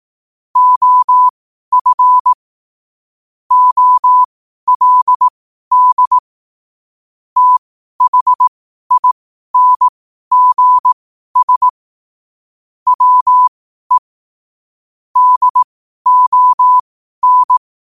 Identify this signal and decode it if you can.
Morse code, heard as a keyed tone, about 18 wpm, spaced as for 12 wpm, 1000 Hz. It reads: OF OLD THINGS WE DON
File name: qotdfiles / 12wpm